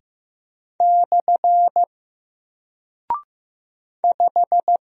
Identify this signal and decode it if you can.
Morse code, heard as a keyed tone, about 15 wpm, 700 Hz. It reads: / 5
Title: Single Character (Letter:Number:Punctuation:Prosign) Alt 15wpm